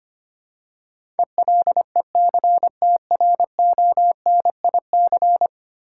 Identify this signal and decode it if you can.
Morse code, heard as a keyed tone, about 25 wpm, 690 Hz. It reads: ELECTRONIC
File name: 10 Letter Words - First 3 Letters Given - 25wpm